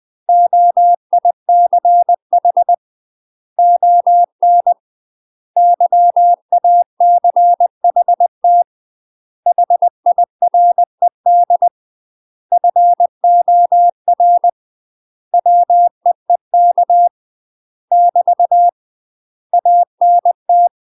Morse code, heard as a keyed tone, about 20 wpm, 700 Hz. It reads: OICH ON YACHT HIRED FOR WEEK <BT> ANT
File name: UK MASTER QSOs TRAINING / 20WPM / UK MASTER QSOs 0062